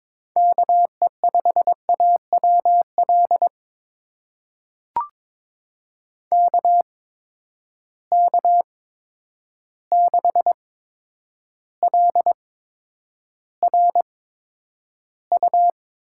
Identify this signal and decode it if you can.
Morse code, heard as a keyed tone, about 22 wpm, spaced as for 9 wpm, 700 Hz. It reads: KE5AWL KK6LRU
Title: USA Call Signs - 8x Character Spacing - 22wpm